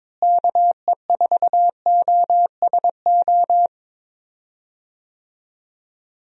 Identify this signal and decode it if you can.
Morse code, heard as a keyed tone, about 22 wpm, 700 Hz. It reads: KE4OSO
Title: USA Call Signs - 6 Characters - Repeated 3x - 22wpm